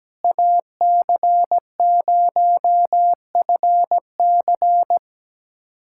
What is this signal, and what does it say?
Morse code, heard as a keyed tone, about 17 wpm, 700 Hz. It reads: AC0FC
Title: USA Callsign DE Callsign - 5 Characters - 17wpm